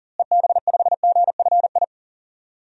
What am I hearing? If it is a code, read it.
Morse code, heard as a keyed tone, about 40 wpm, 700 Hz. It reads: EB5GFI
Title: DXCC-100 Call Signs - 6 Characters - Repeated 2x - 40wpm